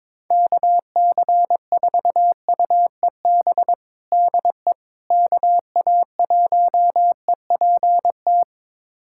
Morse code, heard as a keyed tone, about 22 wpm, 700 Hz. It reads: KC4UEB DE KA1EPT
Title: USA Callsign DE Callsign - 6 Characters - 22wpm